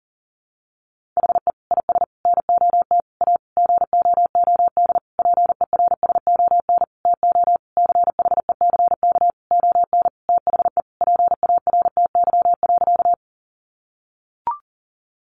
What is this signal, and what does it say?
Morse code, heard as a keyed tone, about 40 wpm, 700 Hz. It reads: HE IS NOT A GOOD PERSON TO CHECK ON THE PARTY.
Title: Sentences from Top 1400 Words 40wpm